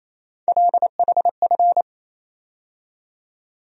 Morse code, heard as a keyed tone, about 28 wpm, 700 Hz. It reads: LHF
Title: USA Call Sign Postfixes - 3 Letters 28wpm